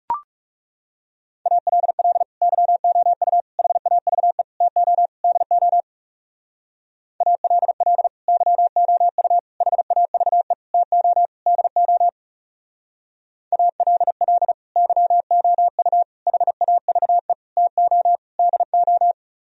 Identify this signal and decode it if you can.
Morse code, heard as a keyed tone, about 36 wpm, 700 Hz. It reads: ALL YOU HAVE TO DO ALL YOU HAVE TO DO ALL YOU HAVE TO DO
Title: Speed-Racing - Sets of 5 Words 30wpm